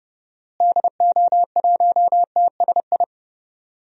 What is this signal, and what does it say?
Morse code, heard as a keyed tone, about 30 wpm, 700 Hz. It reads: DO1TSI
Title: DXCC-100 Call Signs - 6 Characters - Repeated 3x - 30wpm